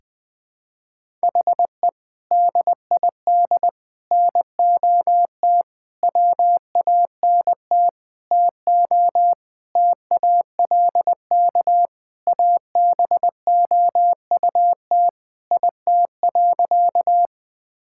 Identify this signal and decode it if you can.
Morse code, heard as a keyed tone, about 20 wpm, 700 Hz. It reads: HE DID NOT WANT TO TALK ABOUT IT.
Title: Sentences with 5-Word Phrases - 20wpm